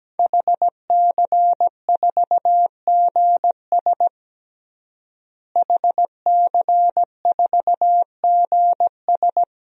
Morse code, heard as a keyed tone, about 17 wpm, 700 Hz. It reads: HC4GS HC4GS
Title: DXCC-100 Call Signs - 5 Characters - Repeated 3x - 17wpm